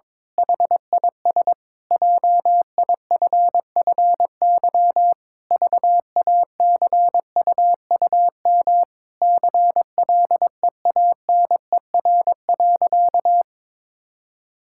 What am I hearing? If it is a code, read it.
Morse code, heard as a keyed tone, about 21 wpm, 700 Hz. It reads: HIS JIFFY VACUUM CLEANER.
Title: Pangram Challenge v2 - 22wpm